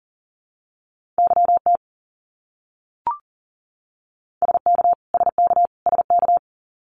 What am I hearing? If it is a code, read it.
Morse code, heard as a keyed tone, about 37 wpm, 700 Hz. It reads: YT SK SK SK
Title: Canadian Provinces and Territories - Repeated 3x - 40wpm